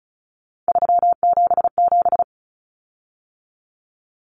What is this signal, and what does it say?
Morse code, heard as a keyed tone, about 34 wpm, 700 Hz. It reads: 377
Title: Common Cut Numbers - 3-Digit 35wpm